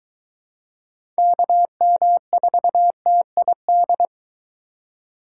Morse code, heard as a keyed tone, about 22 wpm, 700 Hz. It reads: KM4TID